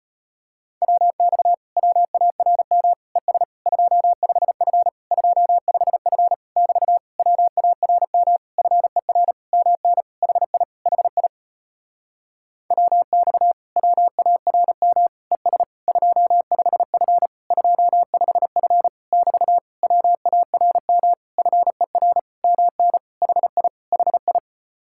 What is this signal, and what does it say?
Morse code, heard as a keyed tone, about 36 wpm, 700 Hz. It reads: WX WARM ES 25F 25F <BT> WARM FER MN HI HI WX WARM ES 25F 25F <BT> WARM FER MN HI HI